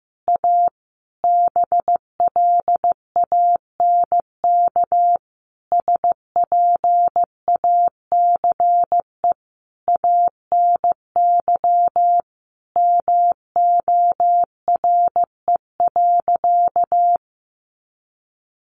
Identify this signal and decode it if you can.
Morse code, heard as a keyed tone, about 15 wpm, 700 Hz. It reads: A BLANK SPACE ANY MORE.
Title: Sentences from Top 1500 Words 15wpm